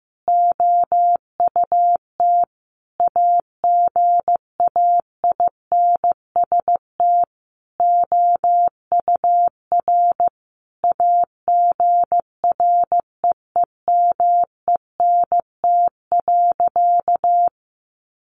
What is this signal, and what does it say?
Morse code, heard as a keyed tone, about 15 wpm, 700 Hz. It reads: OUT AGAINST OUR AGREEMENT.
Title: Sentences from Top 1500 Words 15wpm